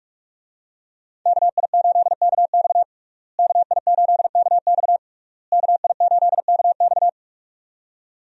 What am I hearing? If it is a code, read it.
Morse code, heard as a keyed tone, about 46 wpm, 700 Hz. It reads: KI8KX KI8KX KI8KX
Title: USA Call Signs - 5 Characters - Repeated 3x - 45wpm